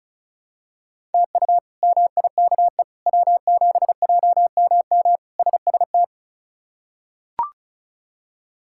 Morse code, heard as a keyed tone, about 34 wpm, 700 Hz. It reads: TU MIKE W7JMM SST
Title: K1USN SST QSO Practice - 35wpm